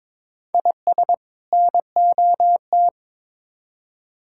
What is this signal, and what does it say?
Morse code, heard as a keyed tone, about 22 wpm, 700 Hz. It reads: IS NOT
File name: Build A Phrase - 2-Word Phrases - 22wpm